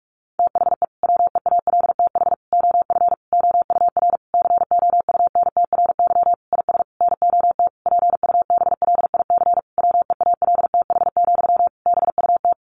Morse code, heard as a keyed tone, about 44 wpm, 700 Hz. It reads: THE WEALTH OF OUR COUNTRY IS NOT PUBLIC WEALTH, BUT
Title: Sentences from Top 1200 Words 45wpm